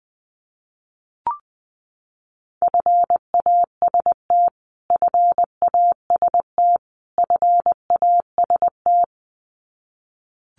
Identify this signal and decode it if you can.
Morse code, heard as a keyed tone, about 20 wpm, 700 Hz. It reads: FAST FAST FAST
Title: Top 500 Words Repeated 20wpm